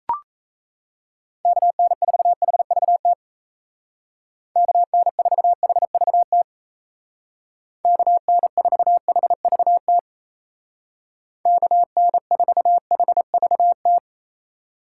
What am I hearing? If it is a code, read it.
Morse code, heard as a keyed tone, about 37 wpm, 700 Hz. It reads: KN4HVT KN4HVT KN4HVT KN4HVT